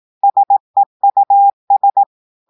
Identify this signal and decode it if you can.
Morse code, heard as a keyed tone, about 18 wpm, 800 Hz. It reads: SEUS
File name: audios / portugais / portugais texte en clair 2 vitesse 18 mots minutes